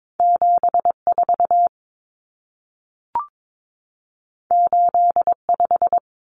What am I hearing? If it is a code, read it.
Morse code, heard as a keyed tone, about 22 wpm, 700 Hz. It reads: 74 85
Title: Numbers - 2-Digit 22wpm